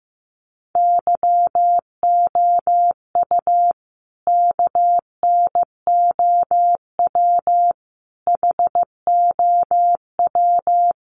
Morse code, heard as a keyed tone, about 15 wpm, 700 Hz. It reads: YOU KNOW HOW
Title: Sentences from Top 1100 Words 15wpm